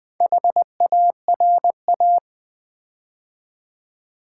Morse code, heard as a keyed tone, about 20 wpm, 700 Hz. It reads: HARA
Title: Build A Word - 4 Letters At A Time - 10-Letter Words - 20wpm